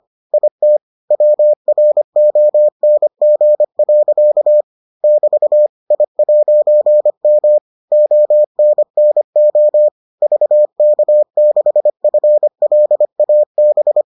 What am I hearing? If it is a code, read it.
Morse code, heard as a keyed tone, about 25 wpm, 600 Hz. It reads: IT WRONG. <BT> I'M ONNO VK6FLAB